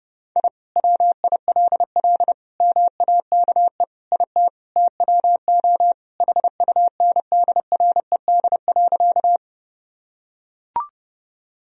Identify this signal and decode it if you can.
Morse code, heard as a keyed tone, about 32 wpm, 700 Hz. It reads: I WILL MAKE IT TWO HUNDRED.
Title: Sentences from Top 1000 Words - Review Part 2 - 30wpm